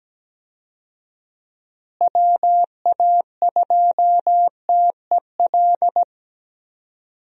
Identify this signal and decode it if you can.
Morse code, heard as a keyed tone, about 17 wpm, 700 Hz. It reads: WA2TEL
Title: USA Call Signs - 6 Characters - Repeated 2x - 17wpm